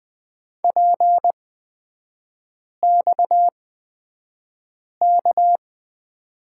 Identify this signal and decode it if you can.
Morse code, heard as a keyed tone, about 20 wpm, spaced as for 5 wpm, 690 Hz. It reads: PXK
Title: 229 Lesson 37e - Letter x - Three Characters 20wpm